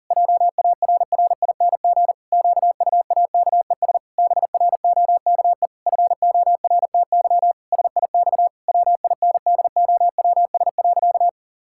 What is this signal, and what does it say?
Morse code, heard as a keyed tone, about 40 wpm, 700 Hz. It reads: JARRING QUAKES BROKE FORTY SIX WINDOWS.
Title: Pangram Challenge v2 - 40wpm